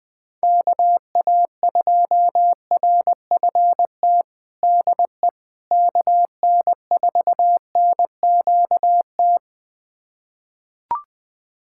Morse code, heard as a keyed tone, about 21 wpm, 700 Hz. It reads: KA2RFT DE KN4NQT
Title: USA Callsign DE Callsign 20wpm